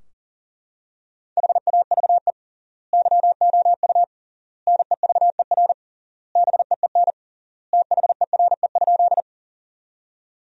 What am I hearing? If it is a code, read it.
Morse code, heard as a keyed tone, about 40 wpm, 700 Hz. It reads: HAVE YOU NEVER BEEN THERE?
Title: 3x Word-Spacing - Sentences from Top 300 Words - 40wpm